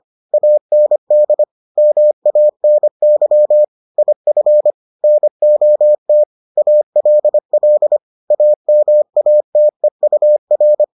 Morse code, heard as a keyed tone, about 25 wpm, 600 Hz. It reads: AND MANY IF NOT ALL AMATEUR